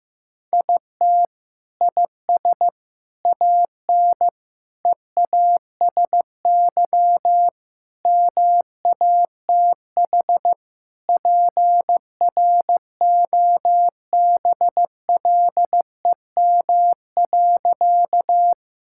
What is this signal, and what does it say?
Morse code, heard as a keyed tone, about 15 wpm, 700 Hz. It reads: IT IS AN EASY MATH PROBLEM.
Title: Sentences from Top 1000 Words - Review Part 6 - 15wpm